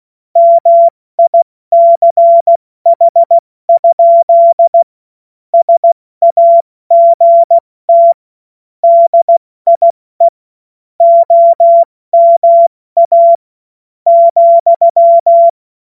Morse code, heard as a keyed tone, about 16 wpm, spaced as for 13 wpm, 680 Hz. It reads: MICH? SAGT DIE OMA,
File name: Morsekurs / Witze / Witz 24. August 2021, 70 F80